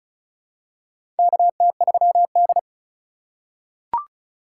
Morse code, heard as a keyed tone, about 36 wpm, 700 Hz. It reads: KT3D
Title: USA Call Signs - 4 Characters - 2x Character Spacing - 35wpm